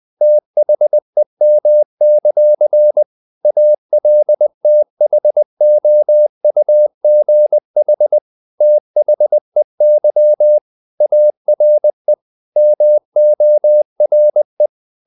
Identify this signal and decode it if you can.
Morse code, heard as a keyed tone, about 20 wpm, 600 Hz. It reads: THEM; ALTHOUGH THEY ARE MORE
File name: the old man and the sea / the old man and the sea part 1 20wpm